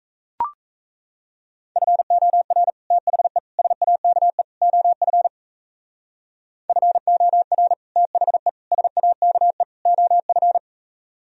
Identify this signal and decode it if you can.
Morse code, heard as a keyed tone, about 35 wpm, 700 Hz. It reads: FOR THE SAKE OF FOR THE SAKE OF